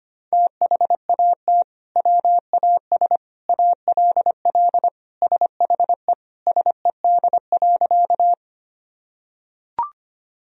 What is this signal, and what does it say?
Morse code, heard as a keyed tone, about 25 wpm, 700 Hz. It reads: THAT WAS ALL SHE SED.
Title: Sentences from Top 100 Words and Abbreviations 25wpm